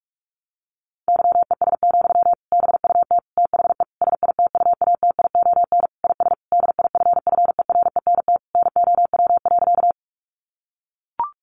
Morse code, heard as a keyed tone, about 44 wpm, 700 Hz. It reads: YES, BUT THE SITUATION IS DIFFERENT NOW.
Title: Sentences from Top 1200 Words 45wpm